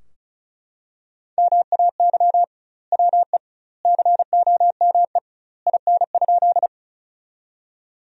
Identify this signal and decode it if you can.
Morse code, heard as a keyed tone, about 34 wpm, 700 Hz. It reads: MAY WE COME IN?
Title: 2x Word-Spacing - Sentences from Top 100 Words - 35wpm